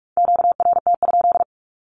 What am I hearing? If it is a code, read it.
Morse code, heard as a keyed tone, about 44 wpm, 700 Hz. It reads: QRT?
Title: Q-Codes 45wpm